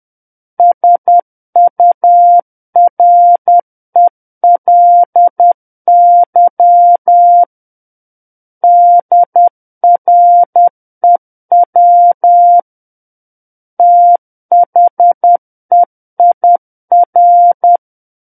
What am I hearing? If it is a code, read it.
Morse code, heard as a keyed tone, about 10 wpm, 700 Hz. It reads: SURELY DREW THEIR
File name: Book One / 01-Chapter 1 - 10 WPM